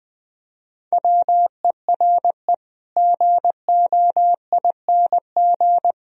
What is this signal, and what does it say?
Morse code, heard as a keyed tone, about 20 wpm, 700 Hz. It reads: WERE GOING